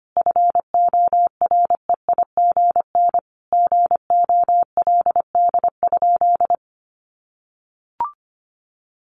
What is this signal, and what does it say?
Morse code, heard as a keyed tone, about 25 wpm, 700 Hz. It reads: FOREIGN GOLD?
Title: Sentences from Top 1500 Words 25wpm